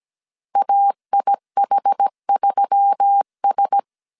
Morse code, heard as a keyed tone, about 17 wpm, 800 Hz. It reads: AIH3S
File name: CW / DET / 1 6 1500C S A 16wpm 2635